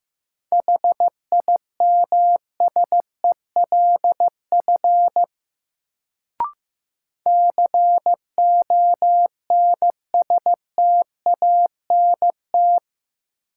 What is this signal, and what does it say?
Morse code, heard as a keyed tone, about 15 wpm, 700 Hz. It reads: HIMSELF CONSTANT
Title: Sentences from Top 1500 Words - New Word Drill Alt - 15wpm